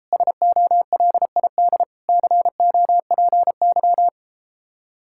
Morse code, heard as a keyed tone, about 33 wpm, 700 Hz. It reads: SOLID COPY